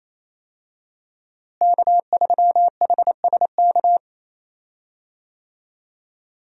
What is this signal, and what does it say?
Morse code, heard as a keyed tone, about 28 wpm, 700 Hz. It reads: K3HSK
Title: USA Call Signs - 5 Characters - Repeated 2x - 28wpm